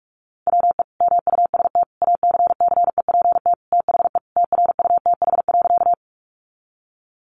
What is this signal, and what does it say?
Morse code, heard as a keyed tone, about 44 wpm, 700 Hz. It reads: WE MUST ACCEPT THE TRUTH.
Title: Sentences from Top 1300 Words 45wpm